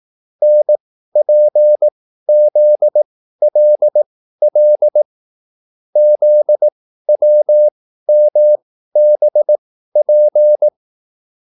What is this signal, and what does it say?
Morse code, heard as a keyed tone, about 18 wpm, spaced as for 13 wpm, 600 Hz. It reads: NPZLL ZWMBP
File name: QSL / CW lesson